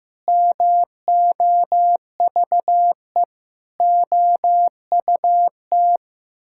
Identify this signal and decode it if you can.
Morse code, heard as a keyed tone, about 15 wpm, 700 Hz. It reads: MOVE OUT
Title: Sentences from Top 1000 Words - Review Part 6 - 15wpm